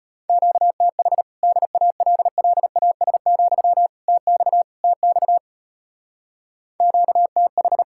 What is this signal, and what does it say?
Morse code, heard as a keyed tone, about 33 wpm, 700 Hz. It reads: QTH DALLAS, TX TX QTH